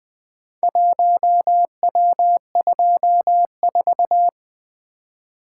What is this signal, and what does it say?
Morse code, heard as a keyed tone, about 20 wpm, 700 Hz. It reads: 1W24
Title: Four Letter-Number Combinations - 20wpm